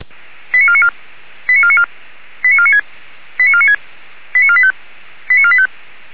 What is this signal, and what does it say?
Five-tone selective call (ZVEI), heard as quick runs of five tones, tones of about 70 ms. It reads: 89494 89494 89496 89496 89485 89485